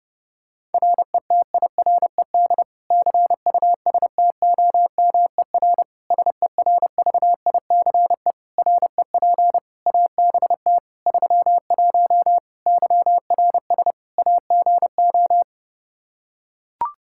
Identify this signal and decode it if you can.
Morse code, heard as a keyed tone, about 30 wpm, 700 Hz. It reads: RETIRED CUSTOMER SERVICE REP ABT 31 YRS AGO
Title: Practice QSO - Exchange 3 - More Info - 30wpm